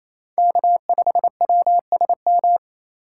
Morse code, heard as a keyed tone, about 36 wpm, 700 Hz. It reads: K5WSM